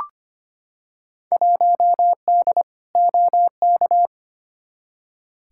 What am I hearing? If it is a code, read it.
Morse code, heard as a keyed tone, about 25 wpm, 700 Hz. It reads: 1D OK
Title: Field Day Exchanges - 25wpm